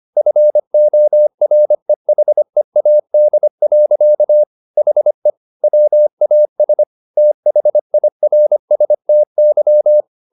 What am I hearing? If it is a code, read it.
Morse code, heard as a keyed tone, about 25 wpm, 600 Hz. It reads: FOREHEAD. HE WAS THIRSTY
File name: the old man and the sea part 1 25wpm